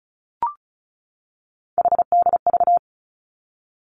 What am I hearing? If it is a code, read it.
Morse code, heard as a keyed tone, about 34 wpm, 700 Hz. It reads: HDV
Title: Three Challenging Letter-Number Combinations 35wpm